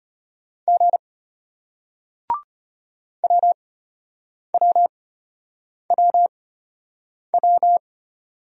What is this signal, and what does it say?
Morse code, heard as a keyed tone, about 33 wpm, spaced as for 8 wpm, 700 Hz. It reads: G WWWW